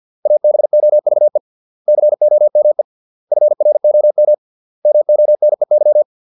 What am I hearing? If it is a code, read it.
Morse code, heard as a keyed tone, about 50 wpm, 600 Hz. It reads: ABOVE COME FROM MONEY